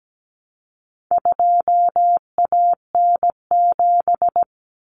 Morse code, heard as a keyed tone, about 17 wpm, 700 Hz. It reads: 2AN7
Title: Common Cut Numbers - 4-Digit 17wpm